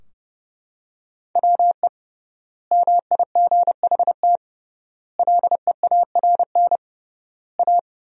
Morse code, heard as a keyed tone, about 30 wpm, 700 Hz. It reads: WE MIGHT LEARN A
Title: 3x Word-Spacing - Sentences from Top 300 Words - 30wpm